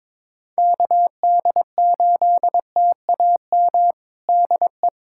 Morse code, heard as a keyed tone, about 22 wpm, 700 Hz. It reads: KD8TAM DE
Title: USA Callsign DE Callsign - 6 Characters - 22wpm